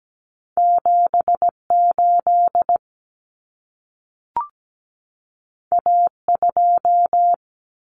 Morse code, heard as a keyed tone, about 17 wpm, 700 Hz. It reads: 78 A2
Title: Common Cut Numbers - 2-Digit 17wpm